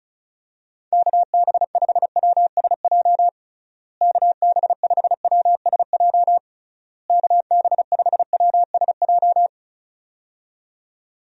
Morse code, heard as a keyed tone, about 34 wpm, 700 Hz. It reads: KB5WSJ KB5WSJ KB5WSJ
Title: USA Call Signs - 6 Characters - Repeated 3x - 35wpm